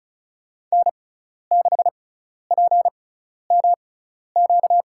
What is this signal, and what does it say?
Morse code, heard as a keyed tone, about 36 wpm, spaced as for 12 wpm, 700 Hz. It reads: N6PMQ
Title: USA Call Signs - 5 Characters - 6x Character Spacing - 35wpm